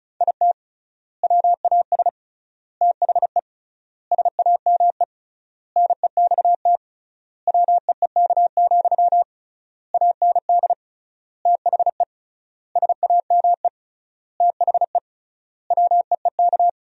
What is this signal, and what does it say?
Morse code, heard as a keyed tone, about 36 wpm, 700 Hz. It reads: IT WAS THE SAME NEXT WEEK, AND THE SAME THE WEEK
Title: 3x Word-Spacing - Sentences from Top 500 Words - 35wpm